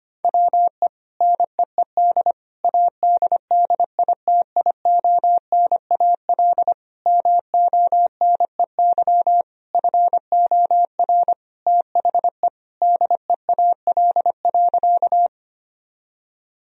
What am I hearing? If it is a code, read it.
Morse code, heard as a keyed tone, about 25 wpm, 700 Hz. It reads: WE NEED ADDITIONAL MONEY FOR THE DEAL.
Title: Sentences from Top 1000 Words - Review Part 4 - 25wpm